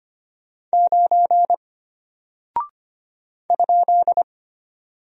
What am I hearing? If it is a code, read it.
Morse code, heard as a keyed tone, about 25 wpm, 700 Hz. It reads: 9 ?
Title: Single Character (Letter:Number:Punctuation:Prosign) Alt 25wpm